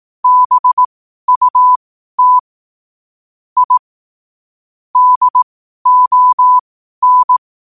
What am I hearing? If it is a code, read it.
Morse code, heard as a keyed tone, about 18 wpm, spaced as for 12 wpm, 1000 Hz. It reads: BUT I DON
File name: qotdfiles / 12wpm